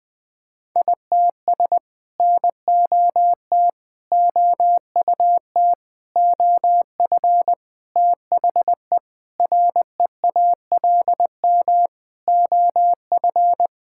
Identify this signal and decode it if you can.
Morse code, heard as a keyed tone, about 20 wpm, 700 Hz. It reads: ITS NOT OUT OF THE REALM OF
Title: Sets of 9 Words - 20wpm